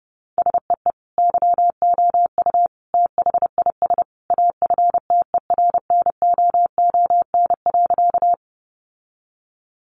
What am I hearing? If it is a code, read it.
Morse code, heard as a keyed tone, about 30 wpm, 700 Hz. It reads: SEE YOU THIS AFTERNOON.
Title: Sentences from Top 1500 Words 30wpm